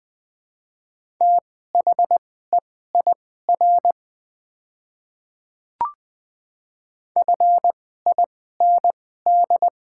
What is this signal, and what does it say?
Morse code, heard as a keyed tone, about 20 wpm, spaced as for 15 wpm, 700 Hz. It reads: THEIR FIND
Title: 2x Character Spacing - Top 100 Words - 20wpm